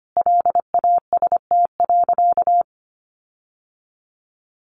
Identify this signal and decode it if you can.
Morse code, heard as a keyed tone, about 25 wpm, 700 Hz. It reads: LAST.
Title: Sentences from Top 1500 Words 25wpm